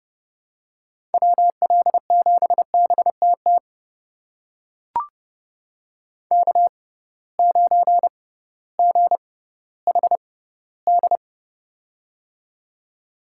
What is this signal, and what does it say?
Morse code, heard as a keyed tone, about 32 wpm, spaced as for 17 wpm, 700 Hz. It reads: WL7BTT K9GHD
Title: USA Call Signs - 6x Character Spacing - 30wpm